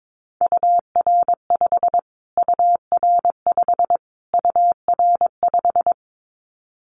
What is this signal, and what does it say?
Morse code, heard as a keyed tone, about 22 wpm, 700 Hz. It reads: UR5 UR5 UR5
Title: Callsign Prefixes for Top 25 DXCC - Repeated 3x - 22wpm